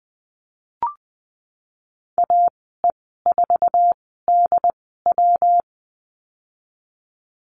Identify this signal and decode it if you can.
Morse code, heard as a keyed tone, about 19 wpm, spaced as for 14 wpm, 700 Hz. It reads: AE4DW
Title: 154 Lesson 25g - Letter p - Call Signs 20wpm